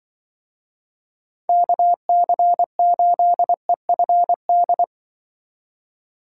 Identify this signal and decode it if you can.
Morse code, heard as a keyed tone, about 23 wpm, 700 Hz. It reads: KC8EFD
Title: Speed Racing - USA Call Signs 20wpm